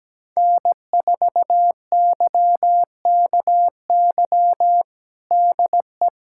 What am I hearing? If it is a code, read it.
Morse code, heard as a keyed tone, about 17 wpm, 700 Hz. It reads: N4YKY DE
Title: USA Callsign DE Callsign 17wpm